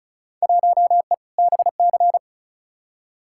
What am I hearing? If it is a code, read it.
Morse code, heard as a keyed tone, about 36 wpm, 700 Hz. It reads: 1E BC
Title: Field Day Exchanges - 35wpm